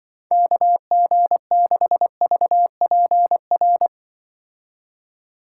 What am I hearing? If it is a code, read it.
Morse code, heard as a keyed tone, about 23 wpm, 700 Hz. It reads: KG6VPR
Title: Speed Racing - USA 2x3 Call Signs 20wpm